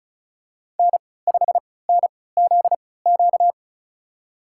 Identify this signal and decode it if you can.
Morse code, heard as a keyed tone, about 36 wpm, spaced as for 20 wpm, 700 Hz. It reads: N5NZQ
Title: USA Call Signs - 5 Characters - 3x Character Spacing - 35wpm